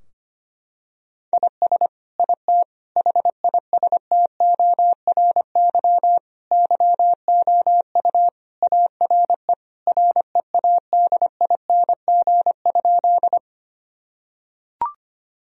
Morse code, heard as a keyed tone, about 25 wpm, 700 Hz. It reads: IS IT HISTORY YOU ARE READING?
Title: Sentences from Top 1000 Words - Review Part 3 - 25wpm